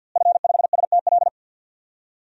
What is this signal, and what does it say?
Morse code, heard as a keyed tone, about 50 wpm, 700 Hz. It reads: F5STL
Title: DXCC-100 Call Signs - 5 Characters - Repeated 3x - 50wpm